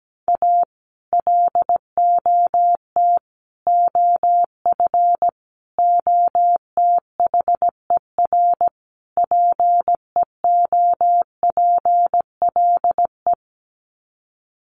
Morse code, heard as a keyed tone, about 17 wpm, 700 Hz. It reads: A LOT OF OTHER PEOPLE
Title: Sets of 5 Words Encore2 17wpm